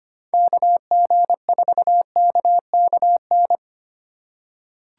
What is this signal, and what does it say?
Morse code, heard as a keyed tone, about 25 wpm, 700 Hz. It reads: KG4KKN
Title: USA Call Signs - 6 Characters - 3x Character Spacing - 25wpm